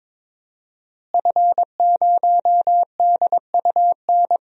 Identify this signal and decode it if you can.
Morse code, heard as a keyed tone, about 22 wpm, 700 Hz. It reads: F0DUN
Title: DXCC-100 Call Signs - 5 Characters - Repeated 2x - 22wpm